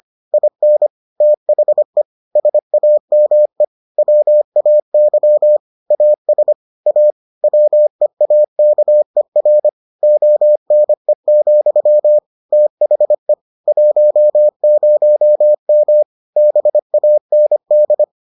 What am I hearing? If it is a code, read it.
Morse code, heard as a keyed tone, about 25 wpm, 600 Hz. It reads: IN THE SAME WAY AS A WEAKER ONE, THE 10M BAND